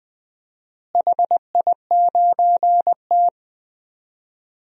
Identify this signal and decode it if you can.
Morse code, heard as a keyed tone, about 20 wpm, 700 Hz. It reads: HI9T
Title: DXCC-100 Call Signs - 4 Characters - Repeated 2x - 20wpm